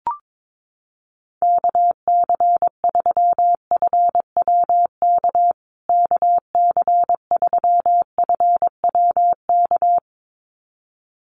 Morse code, heard as a keyed tone, about 22 wpm, 700 Hz. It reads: KC3FWK KC3FWK
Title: USA Call Signs Repeated 2x 22wpm